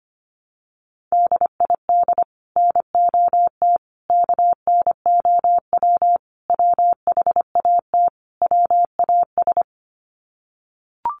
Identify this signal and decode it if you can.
Morse code, heard as a keyed tone, about 25 wpm, 700 Hz. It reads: DID NOT KNOW WHAT WAS
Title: Sets of 5 Words Encore2 25wpm